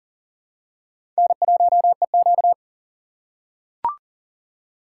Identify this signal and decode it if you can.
Morse code, heard as a keyed tone, about 40 wpm, 700 Hz. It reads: N1EQ
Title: USA Call Signs - 4 Characters - 4x Character Spacing - 40wpm